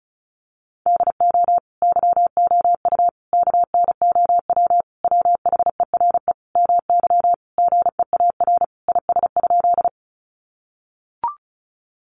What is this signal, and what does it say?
Morse code, heard as a keyed tone, about 34 wpm, 700 Hz. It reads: DO YOU KNOW WHERE MY GEAR IS?
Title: Sentences from Top 1400 Words 35wpm